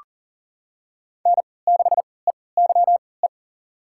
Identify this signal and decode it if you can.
Morse code, heard as a keyed tone, about 38 wpm, spaced as for 22 wpm, 690 Hz. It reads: N6EYE
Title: USA Call Signs - 5 Characters - 3x Character Spacing - 40wpm